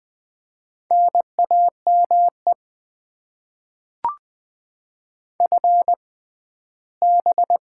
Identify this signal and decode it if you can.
Morse code, heard as a keyed tone, about 20 wpm, spaced as for 14 wpm, 700 Hz. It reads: NAME FB
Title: Top 20 QSO Elements - 6x Character Spacing - 20wpm